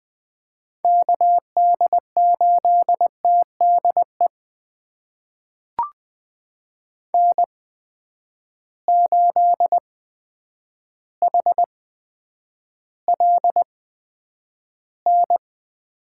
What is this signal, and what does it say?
Morse code, heard as a keyed tone, about 20 wpm, spaced as for 9 wpm, 700 Hz. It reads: KD8TDE N8HLN
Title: USA Call Signs - 8x Character Spacing - 20wpm